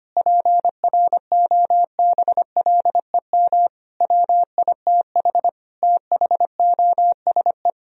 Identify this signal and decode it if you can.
Morse code, heard as a keyed tone, about 25 wpm, 700 Hz. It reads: PROBLEM WITH THOSE
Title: Sentences from Top 1000 Words - Review Part 2 - 25wpm